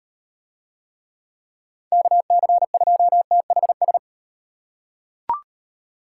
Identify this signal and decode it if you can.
Morse code, heard as a keyed tone, about 38 wpm, 700 Hz. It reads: KC2THS